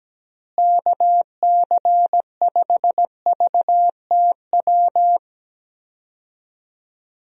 Morse code, heard as a keyed tone, about 17 wpm, 700 Hz. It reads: KC5VTW
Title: USA Call Signs - 6 Characters - Repeated 2x - 17wpm